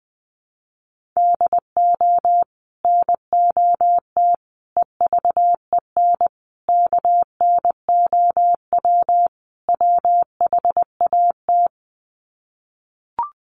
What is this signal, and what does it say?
Morse code, heard as a keyed tone, about 20 wpm, 700 Hz. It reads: DO NOT EVEN KNOW WHAT
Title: Sets of 5 Words Encore 20wpm